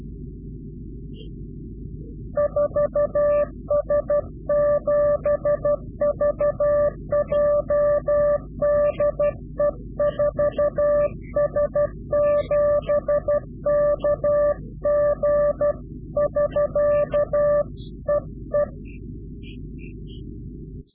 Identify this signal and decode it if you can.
Morse code, heard as a keyed tone, about 12 wpm, 590 Hz. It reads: 4S7VJ DE 4S7KG <SK> EE
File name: CW-QSO / Directional Call / Directional QSO